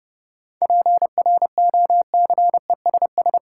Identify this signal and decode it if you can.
Morse code, heard as a keyed tone, about 31 wpm, 700 Hz. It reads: PROCESS
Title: Sentences from Top 1000 Words - Review Part 3 - New Word Drill - 30wpm